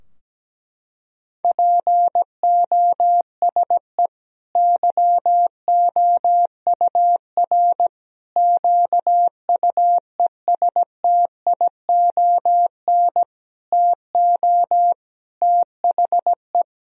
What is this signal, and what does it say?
Morse code, heard as a keyed tone, about 17 wpm, 700 Hz. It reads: POSE YOUR QUESTION TO THE
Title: Sentences from Top 1000 Words - Review Part 2 - 17wpm